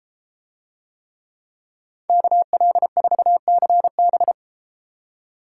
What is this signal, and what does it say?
Morse code, heard as a keyed tone, about 32 wpm, 700 Hz. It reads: KL4CB